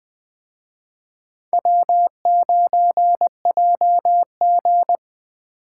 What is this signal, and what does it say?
Morse code, heard as a keyed tone, about 23 wpm, 700 Hz. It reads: W9JG
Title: Speed Racing - USA 1x2 Call Signs 20wpm